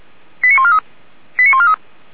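Five-tone selective call (ZVEI), heard as quick runs of five tones, tones of about 70 ms. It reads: 89143 89143
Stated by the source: ZVEI five-tone selective call received by SDR.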